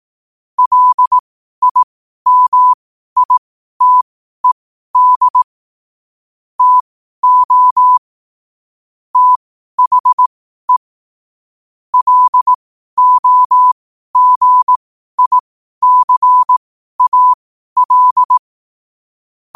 Morse code, heard as a keyed tone, about 18 wpm, spaced as for 12 wpm, 1000 Hz. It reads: LIMITED TO THE LOGICAL
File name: qotdfiles / 12wpm